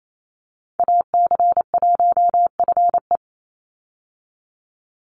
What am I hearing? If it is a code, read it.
Morse code, heard as a keyed tone, about 28 wpm, 700 Hz. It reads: AC1FE
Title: Call Signs Repeated 3x - 28wpm